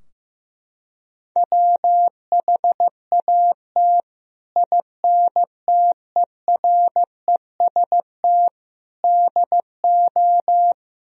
Morse code, heard as a keyed tone, about 15 wpm, 700 Hz. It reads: WHAT INTEREST DO
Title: Sentences from Top 1000 Words - Review Part 2 - 15wpm